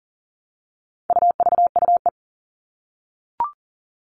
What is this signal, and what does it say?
Morse code, heard as a keyed tone, about 40 wpm, 700 Hz. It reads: UVUE
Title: All Cut Numbers - 4-Digit 40wpm